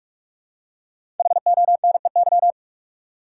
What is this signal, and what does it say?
Morse code, heard as a keyed tone, about 43 wpm, 700 Hz. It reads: HONEY
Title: Five Letter Words - 45wpm